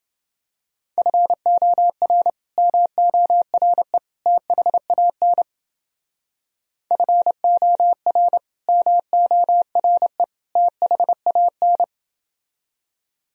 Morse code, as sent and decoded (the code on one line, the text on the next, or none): ..-. --- .-. / -- --- .-. / - .... .- -. / ..-. --- .-. / -- --- .-. / - .... .- -.
FOR MORE THAN FOR MORE THAN